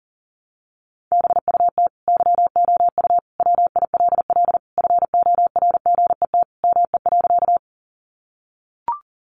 Morse code, heard as a keyed tone, about 40 wpm, 700 Hz. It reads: BUT YOU WILL FORGET ME.
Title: Sentences from Top 1200 Words 40wpm